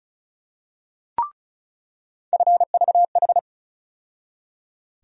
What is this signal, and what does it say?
Morse code, heard as a keyed tone, about 36 wpm, 700 Hz. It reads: FVH
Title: USA Call Sign Postfixes - 3 Letters 35wpm